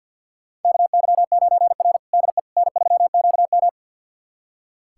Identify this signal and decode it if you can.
Morse code, heard as a keyed tone, about 46 wpm, 700 Hz. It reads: KY9R DE N3QM
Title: USA Callsign DE Callsign - 4 Characters - 50wpm